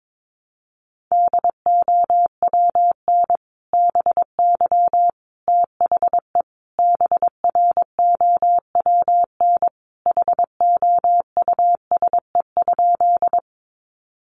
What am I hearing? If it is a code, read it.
Morse code, heard as a keyed tone, about 22 wpm, 700 Hz. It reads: DOWN BY THE BROWN HOUSE?
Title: Sentences from Top 1200 Words 22wpm